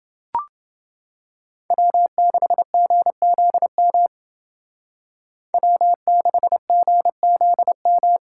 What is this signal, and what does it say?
Morse code, heard as a keyed tone, about 29 wpm, 700 Hz. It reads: W6GZM W6GZM